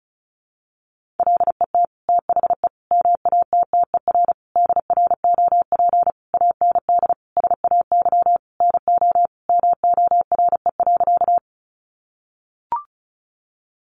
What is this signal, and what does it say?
Morse code, heard as a keyed tone, about 34 wpm, 700 Hz. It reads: LET THE MATTER DROP AND SAY NO MORE.
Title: Sentences from Top 1400 Words 35wpm